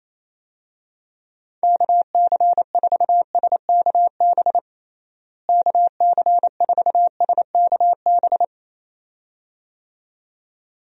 Morse code, heard as a keyed tone, about 28 wpm, 700 Hz. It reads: KC4SKB KC4SKB
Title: USA Call Signs - 6 Characters - Repeated 2x - 28wpm